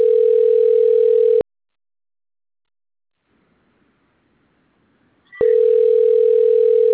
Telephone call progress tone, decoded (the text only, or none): ringback tone